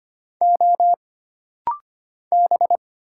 Morse code, heard as a keyed tone, about 25 wpm, 700 Hz. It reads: O B
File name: Single Character (Letter:Number:Punctuation:Prosign) - Rapid-Fire - 25wpm